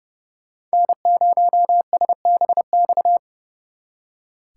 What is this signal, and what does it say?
Morse code, heard as a keyed tone, about 39 wpm, 700 Hz. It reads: N0SBX